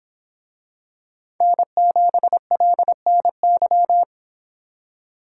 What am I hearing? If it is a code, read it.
Morse code, heard as a keyed tone, about 25 wpm, 700 Hz. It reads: N7LNY